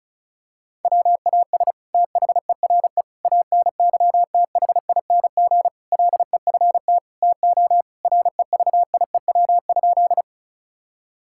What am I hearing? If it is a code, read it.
Morse code, heard as a keyed tone, about 34 wpm, 700 Hz. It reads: WAS THERE ANYTHING LEFT TO REVIEW?
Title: Sentences from Top 1000 Words - Review Part 6 - 35wpm